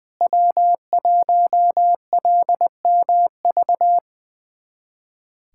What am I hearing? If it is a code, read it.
Morse code, heard as a keyed tone, about 20 wpm, 700 Hz. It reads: W1LMV
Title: USA Callsign DE Callsign - 5 Characters - 20wpm